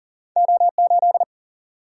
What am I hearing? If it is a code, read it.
Morse code, heard as a keyed tone, about 40 wpm, 700 Hz. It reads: O8
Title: Single-Letter Single-Number 40wpm